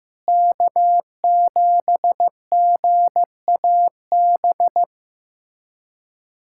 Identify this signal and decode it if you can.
Morse code, heard as a keyed tone, about 15 wpm, 700 Hz. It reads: K7GAB
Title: USA Callsign DE Callsign - 5 Characters - 15wpm